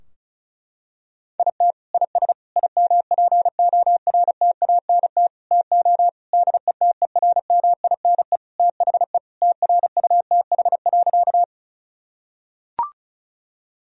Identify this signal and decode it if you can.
Morse code, heard as a keyed tone, about 36 wpm, 700 Hz. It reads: IT IS IMPORTANT TO DETERMINE THE TRUTH.
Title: Sentences from Top 1000 Words - Review Part 4 - 35wpm